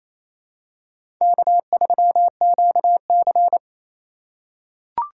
Morse code, heard as a keyed tone, about 28 wpm, 700 Hz. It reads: K3QC
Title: USA Call Signs - 4 Characters - 3x Character Spacing - 28wpm